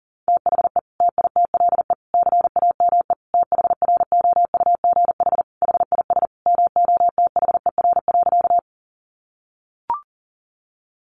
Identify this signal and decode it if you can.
Morse code, heard as a keyed tone, about 37 wpm, 700 Hz. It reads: THE TITLE CAME THROUGH HIS MOTHER.
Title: Sentences from Top 1200 Words 40wpm